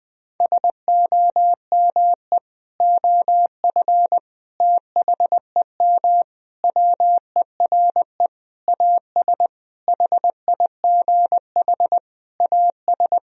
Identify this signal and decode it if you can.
Morse code, heard as a keyed tone, about 20 wpm, 700 Hz. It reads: SOME OF THEM WERE AS HIGH AS
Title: Sentences from Top 1000 Words - Review Part 1 - 20wpm